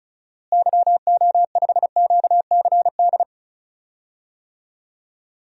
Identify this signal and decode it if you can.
Morse code, heard as a keyed tone, about 34 wpm, 700 Hz. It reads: YO5QCD
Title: DXCC-100 Call Signs - 6 Characters - Repeated 2x - 35wpm